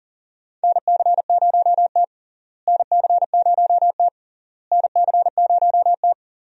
Morse code, heard as a keyed tone, about 39 wpm, 700 Hz. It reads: NC0T NC0T NC0T
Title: USA Call Signs - 4 Characters - Repeated 3x - 40wpm